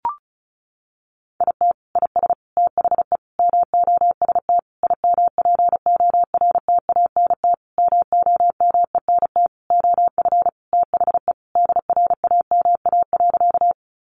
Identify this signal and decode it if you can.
Morse code, heard as a keyed tone, about 34 wpm, 700 Hz. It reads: IT IS THE MOST IMPORTANT MOMENT OF THE DRAMA.
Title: Sentences from Top 1500 Words 35wpm